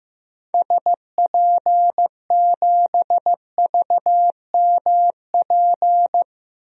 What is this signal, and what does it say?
Morse code, heard as a keyed tone, about 15 wpm, 700 Hz. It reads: SP7VMP
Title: DXCC-100 Call Signs - 6 Characters - Repeated 3x - 15wpm